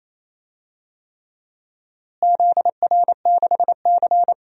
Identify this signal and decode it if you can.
Morse code, heard as a keyed tone, about 28 wpm, 700 Hz. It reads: ZR6C
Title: DXCC-100 Call Signs - 4 Characters - Repeated 3x - 28wpm